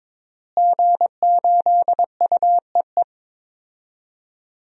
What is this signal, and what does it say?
Morse code, heard as a keyed tone, about 23 wpm, 700 Hz. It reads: G8UEE
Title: DXCC-100 Call Signs - 5 Characters - Repeated 3x - 22wpm